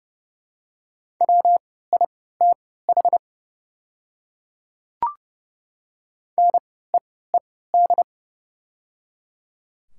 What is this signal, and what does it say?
Morse code, heard as a keyed tone, about 30 wpm, spaced as for 17 wpm, 690 Hz. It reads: WITH NEED
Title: 3x Character Spacing - Top 300 Words - 30wpm